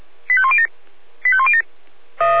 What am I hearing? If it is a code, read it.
Five-tone selective call (ZVEI), heard as quick runs of five tones, tones of about 70 ms. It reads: 85107 85107